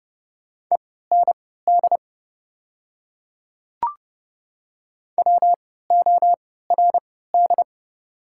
Morse code, heard as a keyed tone, about 30 wpm, spaced as for 17 wpm, 690 Hz. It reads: END WORD
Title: 3x Character Spacing - Top 200 Words - 30wpm